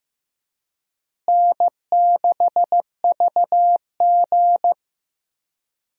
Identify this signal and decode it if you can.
Morse code, heard as a keyed tone, about 15 wpm, 700 Hz. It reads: N6VG
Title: USA Call Signs - 4 Characters - Repeated 3x - 15wpm